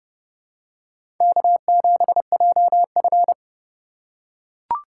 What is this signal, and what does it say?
Morse code, heard as a keyed tone, about 30 wpm, 700 Hz. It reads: K7JF
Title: USA Call Signs - 4 Characters - 2x Character Spacing - 30wpm